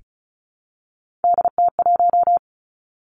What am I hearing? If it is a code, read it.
Morse code, heard as a keyed tone, about 36 wpm, 700 Hz. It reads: DT1
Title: Three Challenging Letter-Number Combinations 35wpm